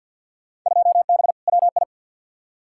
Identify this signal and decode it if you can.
Morse code, heard as a keyed tone, about 50 wpm, 700 Hz. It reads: 2B WI
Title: Field Day Exchanges - 50wpm